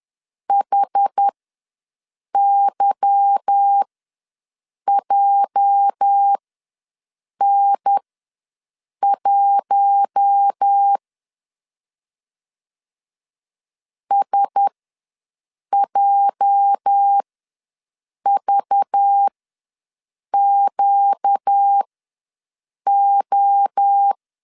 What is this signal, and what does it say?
Morse code, heard as a keyed tone, about 10 wpm, spaced as for 6 wpm, 800 Hz. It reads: HYJN1 SJVQO
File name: LEZIONI / 8 YJ45 / 8 YJ4 ws 10wpm 4020